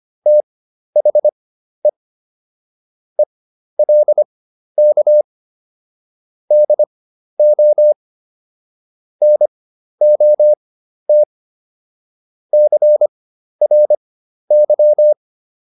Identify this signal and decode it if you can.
Morse code, heard as a keyed tone, about 25 wpm, spaced as for 12 wpm, 600 Hz. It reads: THE ELK DO NOT CRY